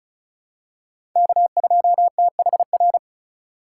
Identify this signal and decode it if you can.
Morse code, heard as a keyed tone, about 36 wpm, 700 Hz. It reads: K2THR
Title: USA Call Signs - 5 Characters - 2x Character Spacing - 35wpm